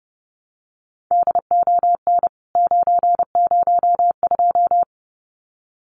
Morse code, heard as a keyed tone, about 30 wpm, 700 Hz. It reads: DON 902
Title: CWT Exchange 30wpm